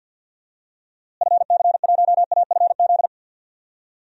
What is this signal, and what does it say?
Morse code, heard as a keyed tone, about 50 wpm, 700 Hz. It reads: FC1AFZ
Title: DXCC-100 Call Signs - 6 Characters - Repeated 2x - 50wpm